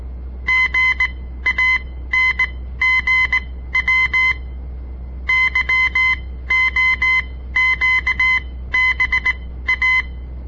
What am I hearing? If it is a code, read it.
Morse code, heard as a keyed tone, about 18 wpm, spaced as for 13 wpm, 2050 Hz. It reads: GANGW YOQBA
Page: On the air Morse code practice from the Southern Montana Amatuer Radio Association; all code, no read back